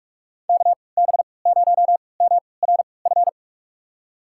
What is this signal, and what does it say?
Morse code, heard as a keyed tone, about 46 wpm, spaced as for 25 wpm, 700 Hz. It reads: KB0MRF
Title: USA Call Signs - 3x Character Spacing - 45wpm